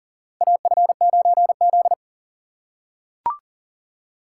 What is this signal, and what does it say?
Morse code, heard as a keyed tone, about 39 wpm, 690 Hz. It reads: AF9Z
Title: USA Call Signs - 4 Characters - 2x Character Spacing - 40wpm